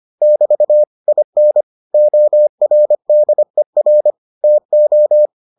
Morse code, heard as a keyed tone, about 25 wpm, 600 Hz. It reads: <BT> IN ORDER TO